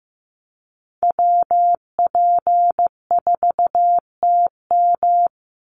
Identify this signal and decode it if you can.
Morse code, heard as a keyed tone, about 15 wpm, 700 Hz. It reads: WP4TM
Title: USA Call Signs 15wpm